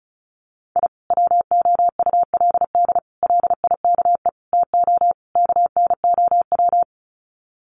Morse code, heard as a keyed tone, about 34 wpm, 700 Hz. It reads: I WOULD LIKE TO KNOW
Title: Sets of 5 Words Encore 35wpm